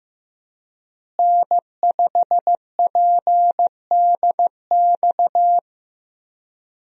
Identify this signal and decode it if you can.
Morse code, heard as a keyed tone, about 15 wpm, 700 Hz. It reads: N5PDX
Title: USA Call Signs - 5 Characters - Repeated 3x - 15wpm